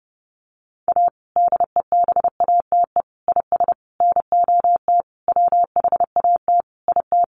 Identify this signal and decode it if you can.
Morse code, heard as a keyed tone, about 30 wpm, 700 Hz. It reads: A DEBATE IS NOT WHAT IT
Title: Sentences from Top 1100 Words 30wpm